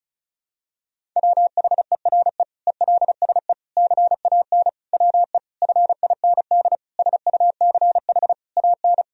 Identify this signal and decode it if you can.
Morse code, heard as a keyed tone, about 34 wpm, 700 Hz. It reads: WHERE ELSE CAN WE FIND SUCH AN
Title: Sentences from Top 1000 Words - Review Part 1 - 35wpm